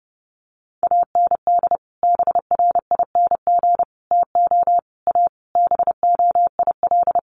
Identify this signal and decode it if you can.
Morse code, heard as a keyed tone, about 30 wpm, 700 Hz. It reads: AND BRING TO A BOIL
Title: Sets of 5 Words Encore 30wpm